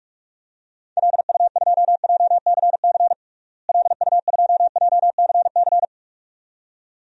Morse code, heard as a keyed tone, about 46 wpm, 700 Hz. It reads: LU2JCC LU2JCC
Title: DXCC-100 Call Signs - 6 Characters - Repeated 2x - 45wpm